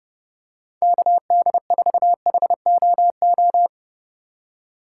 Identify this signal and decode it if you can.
Morse code, heard as a keyed tone, about 29 wpm, 700 Hz. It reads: KD4HOO
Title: USA Call Signs - 6 Characters - 2x Character Spacing - 30wpm